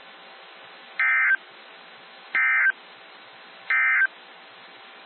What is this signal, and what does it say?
Emergency Alert System (SAME) tones, heard as harsh buzzing bursts. It is heard from 0.9 to 4.1 s.
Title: NOAA Weather Radio - WXJ73